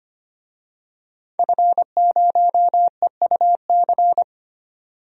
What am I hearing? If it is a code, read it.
Morse code, heard as a keyed tone, about 25 wpm, 700 Hz. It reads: F0EUC
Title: DXCC-100 Call Signs - 5 Characters - 25wpm